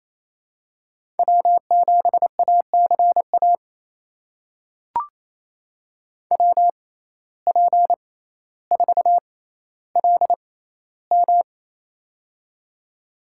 Morse code, heard as a keyed tone, about 28 wpm, spaced as for 14 wpm, 700 Hz. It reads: W7ACA WP4LM
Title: USA Call Signs - 5 Characters - 6x Character Spacing - 28wpm